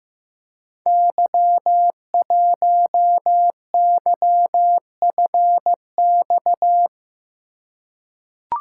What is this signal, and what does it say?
Morse code, heard as a keyed tone, about 15 wpm, 700 Hz. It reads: Y1YFX
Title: Code Groups - 5 Letter:Numbers - 15wpm